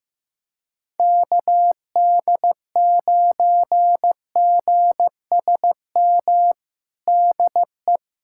Morse code, heard as a keyed tone, about 15 wpm, 700 Hz. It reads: KD9GSM DE
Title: USA Callsign DE Callsign 15wpm